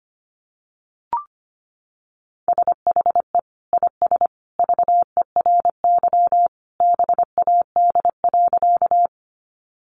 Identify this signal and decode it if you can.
Morse code, heard as a keyed tone, about 25 wpm, 700 Hz. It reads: SHE IS VERY BAD.
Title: Sentences from Top 700 Words 25wpm